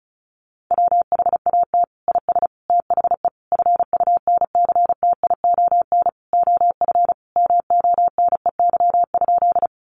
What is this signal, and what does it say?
Morse code, heard as a keyed tone, about 34 wpm, 700 Hz. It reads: WHAT IS THE FUNCTION OF MONEY?
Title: Sentences from Top 1100 Words 35wpm